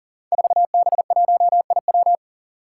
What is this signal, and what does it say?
Morse code, heard as a keyed tone, about 40 wpm, 700 Hz. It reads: 4B1IW
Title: Five Challenging Letter-Number Combinations 40wpm